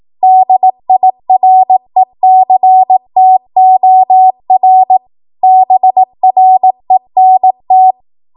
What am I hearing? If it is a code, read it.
Morse code, heard as a keyed tone, about 18 wpm, 750 Hz. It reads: DIRECTOR BRENT